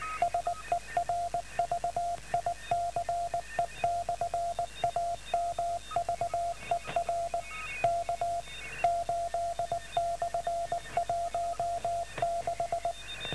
Morse code, heard as a keyed tone, about 19 wpm, 700 Hz. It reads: SERVICE/AMVER K 8/16